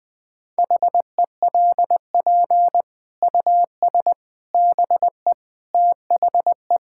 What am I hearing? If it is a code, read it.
Morse code, heard as a keyed tone, about 20 wpm, 700 Hz. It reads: HELP US BE THE
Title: Sentences with 8-Word Phrases - 20wpm